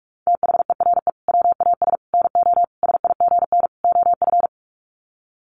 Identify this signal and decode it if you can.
Morse code, heard as a keyed tone, about 44 wpm, 700 Hz. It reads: THERE WAS NO SIGN OF
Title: Sets of 5 Words Encore2 45wpm